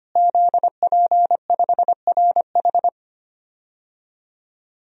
Morse code, heard as a keyed tone, about 25 wpm, 700 Hz. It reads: ZP5RH
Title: DXCC-100 Call Signs - 5 Characters - Repeated 3x - 25wpm